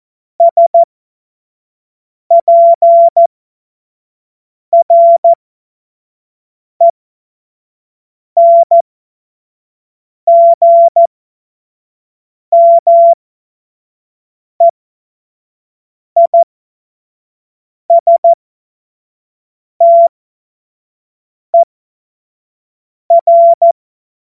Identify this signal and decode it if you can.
Morse code, heard as a keyed tone, about 14 wpm, spaced as for 5 wpm, 680 Hz. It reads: SPRENGMEISTER
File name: Morsekurs / Witze / Witz 08. November 2022 25 F70